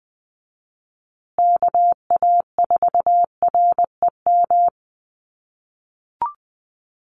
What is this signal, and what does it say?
Morse code, heard as a keyed tone, about 20 wpm, 700 Hz. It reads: KA4REM
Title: USA Call Signs Repeated 2x 20wpm